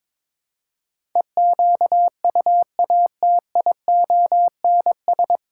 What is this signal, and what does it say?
Morse code, heard as a keyed tone, about 23 wpm, 700 Hz. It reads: EQUATIONS
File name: practice / 22wpm / Build A Phrase - 5-Word Phrases - 22wpm